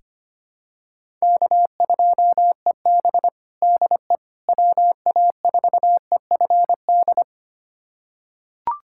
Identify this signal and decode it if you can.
Morse code, heard as a keyed tone, about 25 wpm, 700 Hz. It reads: K2EB DE WA4EFD
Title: USA Callsign DE Callsign 25wpm